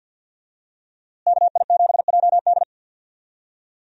Morse code, heard as a keyed tone, about 52 wpm, 700 Hz. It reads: KI7JD
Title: USA Call Signs - 5 Characters - Repeated 2x - 50wpm